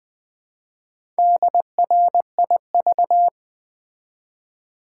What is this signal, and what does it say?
Morse code, heard as a keyed tone, about 20 wpm, 700 Hz. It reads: DRIV
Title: Build A Word - 1 Letter At A Time - 5-Letter Words - 20wpm